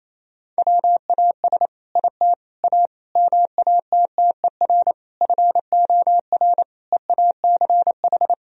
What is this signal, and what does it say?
Morse code, heard as a keyed tone, about 28 wpm, 700 Hz. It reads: WAS IT A MATTER FOR EACH
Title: Sentences from Top 1000 Words - Review Part 4 - 28wpm